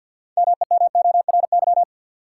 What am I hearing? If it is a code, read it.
Morse code, heard as a keyed tone, about 53 wpm, 700 Hz. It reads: MEMORY
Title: Sentences from Top 1300 Words - New Word Drill Alt - 50wpm